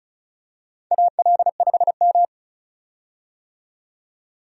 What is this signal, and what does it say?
Morse code, heard as a keyed tone, about 36 wpm, 700 Hz. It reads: AL5M
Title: USA Call Signs - 4 Characters - 35wpm